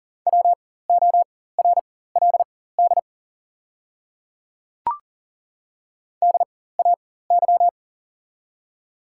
Morse code, heard as a keyed tone, about 39 wpm, spaced as for 18 wpm, 690 Hz. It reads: WORLD DAY
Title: 4x Character Spacing - Top 200 Words - 40wpm